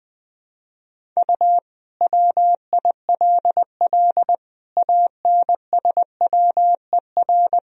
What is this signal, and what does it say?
Morse code, heard as a keyed tone, about 20 wpm, 700 Hz. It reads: U WILL ANSWER
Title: Sentences from Top 300 Words and Abbreviations 20wpm